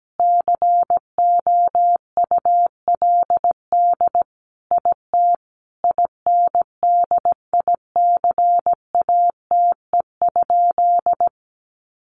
Morse code, heard as a keyed tone, about 17 wpm, 700 Hz. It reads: COULD IT INDICATE?
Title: Sentences from Top 1300 Words 17wpm